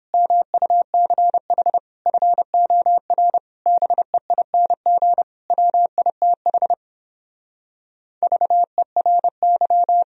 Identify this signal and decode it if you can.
Morse code, heard as a keyed tone, about 31 wpm, 700 Hz. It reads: MUCH FOR BEING WITH VERY